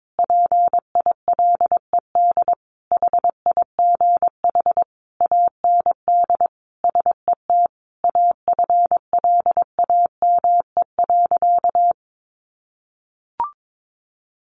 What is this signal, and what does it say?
Morse code, heard as a keyed tone, about 22 wpm, 700 Hz. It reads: PILED HIGH AND SET AFLAME.
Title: A Sound Of Thunder 22wpm